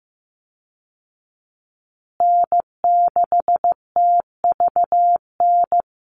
Morse code, heard as a keyed tone, about 15 wpm, 700 Hz. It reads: N6TVN
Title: Call Signs Repeated 2x - 15wpm